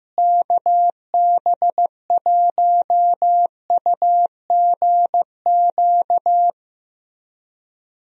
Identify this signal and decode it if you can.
Morse code, heard as a keyed tone, about 15 wpm, 700 Hz. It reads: KB1UGQ
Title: USA Call Signs - 6 Characters - Repeated 3x - 15wpm